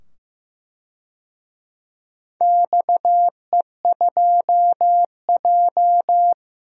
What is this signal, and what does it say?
Morse code, heard as a keyed tone, about 16 wpm, 700 Hz. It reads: XE2J
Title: DXCC-100 Call Signs - 4 Characters - 15wpm